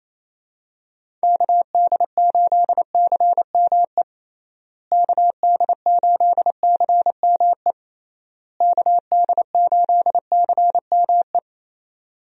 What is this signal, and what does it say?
Morse code, heard as a keyed tone, about 28 wpm, 700 Hz. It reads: KD8CME KD8CME KD8CME
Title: USA Call Signs - 6 Characters - Repeated 3x - 28wpm